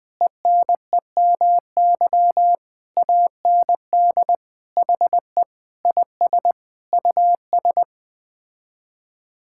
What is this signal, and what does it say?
Morse code, heard as a keyed tone, about 20 wpm, 700 Hz. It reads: ENEMY AND HE IS US
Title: Sets of 9 Words - 20wpm